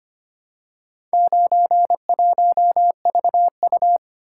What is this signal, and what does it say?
Morse code, heard as a keyed tone, about 25 wpm, 700 Hz. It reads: 91VU
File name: practice / 25wpm / Four Letter-Number Combinations - 25wpm